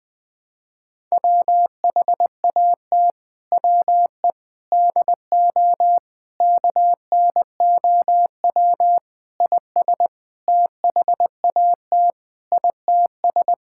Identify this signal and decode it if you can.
Morse code, heard as a keyed tone, about 21 wpm, 700 Hz. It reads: WHAT WE DO KNOW IS THAT ITS
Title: Sentences with 6-Word Phrases - 20wpm